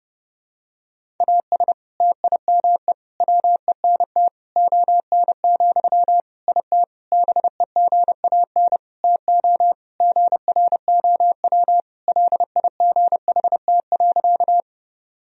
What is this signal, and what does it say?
Morse code, heard as a keyed tone, about 32 wpm, 700 Hz. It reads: AS TIME WENT ON, IT BEGAN TO GROW LIGHT.
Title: Sentences from Top 1000 Words - Review Part 1 - 30wpm